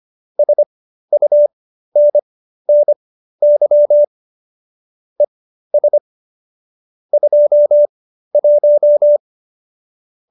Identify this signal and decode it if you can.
Morse code, heard as a keyed tone, about 25 wpm, spaced as for 13 wpm, 590 Hz. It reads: SUNNY ES 21